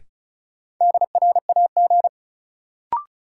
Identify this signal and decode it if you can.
Morse code, heard as a keyed tone, about 36 wpm, 700 Hz. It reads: DRAG
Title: Sentences from Top 1500 Words - New Word Drill - 35wpm